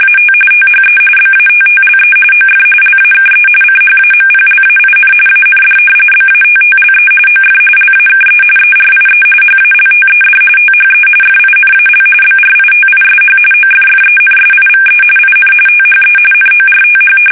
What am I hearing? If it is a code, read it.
RTTY, about 45.45 baud. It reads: AT THE BATTLE OF LAKE CHAMPLAIN AND BATTLE OF LAKE ERIE HALTED THE FINAL BRITISH OFFENSIVE IN THE NORTH
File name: fsk850 / usnavy2